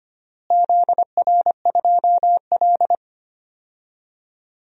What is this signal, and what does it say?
Morse code, heard as a keyed tone, about 25 wpm, 700 Hz. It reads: ZR2L
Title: DXCC-100 Call Signs - 4 Characters - Repeated 2x - 25wpm